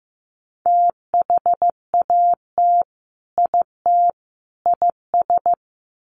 Morse code, heard as a keyed tone, about 15 wpm, 700 Hz. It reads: THAT IT IS
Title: Sets of 3 Words 15wpm